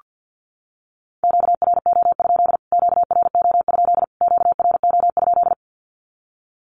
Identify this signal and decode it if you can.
Morse code, heard as a keyed tone, about 49 wpm, 700 Hz. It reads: QRO? QRO? QRO?
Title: Q-Codes Repeated 3x 50wpm